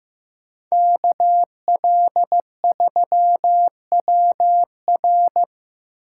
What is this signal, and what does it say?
Morse code, heard as a keyed tone, about 15 wpm, 700 Hz. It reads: KL3WR
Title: USA Call Signs - 5 Characters - 3x Character Spacing - 15wpm